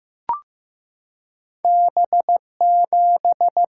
Morse code, heard as a keyed tone, about 15 wpm, 700 Hz. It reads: B7
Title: Single-Letter Single-Number 15wpm